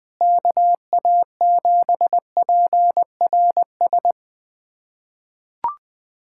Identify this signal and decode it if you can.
Morse code, heard as a keyed tone, about 20 wpm, 700 Hz. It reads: KA7PRS
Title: USA Callsign DE Callsign - 6 Characters - 20wpm